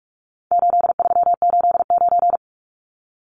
Morse code, heard as a keyed tone, about 44 wpm, 700 Hz. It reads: 8389
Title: Numbers - 4-Digit 45wpm